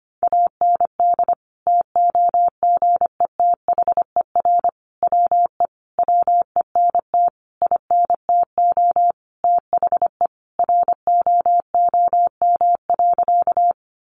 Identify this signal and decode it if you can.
Morse code, heard as a keyed tone, about 25 wpm, 700 Hz. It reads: AND TOGETHER WE WENT INTO THE ROOM.
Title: Sentences from Top 600 Words 25wpm